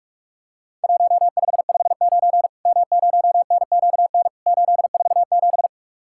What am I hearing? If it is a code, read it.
Morse code, heard as a keyed tone, about 46 wpm, 700 Hz. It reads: 1559 M0NQN 847
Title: Worked All Europe DX Contest - QTC Exchange - 45wpm